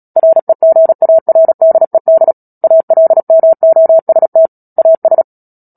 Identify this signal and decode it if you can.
Morse code, heard as a keyed tone, about 37 wpm, 650 Hz. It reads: REGARDED ALMOST AS
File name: Book Two / 10-Chapter 10 - Epilogue - 36 WPM